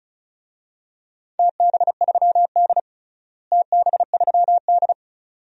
Morse code, heard as a keyed tone, about 36 wpm, 700 Hz. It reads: TB3D TB3D
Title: DXCC-100 Call Signs - 4 Characters - Repeated 2x - 35wpm